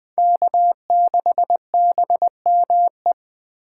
Morse code, heard as a keyed tone, about 20 wpm, 700 Hz. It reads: K6BME
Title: USA Call Signs - 5 Characters - 4x Character Spacing - 20wpm